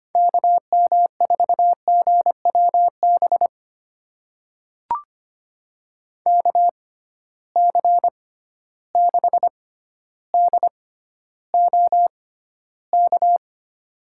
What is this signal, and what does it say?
Morse code, heard as a keyed tone, about 25 wpm, spaced as for 13 wpm, 700 Hz. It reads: KM4GWB KC6DOK
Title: USA Call Signs - 6 Characters - 6x Character Spacing - 25wpm